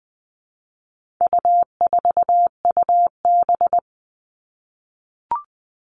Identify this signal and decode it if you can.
Morse code, heard as a keyed tone, about 20 wpm, 700 Hz. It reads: U4UB
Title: All Cut Numbers - 4-Digit 20wpm